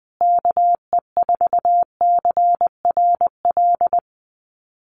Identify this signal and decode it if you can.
Morse code, heard as a keyed tone, about 20 wpm, 700 Hz. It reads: KE4CRL
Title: USA Call Signs Repeated 3x 20wpm